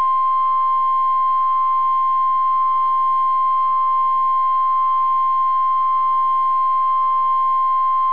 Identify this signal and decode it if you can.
Emergency Alert System (SAME) tones (the 1050 Hz attention tone). It is heard from 0.6 to 7.9 s.